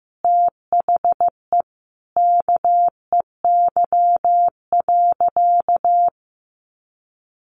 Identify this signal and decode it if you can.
Morse code, heard as a keyed tone, about 15 wpm, 700 Hz. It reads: THE KEY.
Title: Sentences from Top 600 Words 15wpm